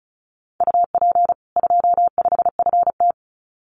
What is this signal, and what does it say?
Morse code, heard as a keyed tone, about 34 wpm, 700 Hz. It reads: UP 25FT
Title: Common QSO Phrases 35wpm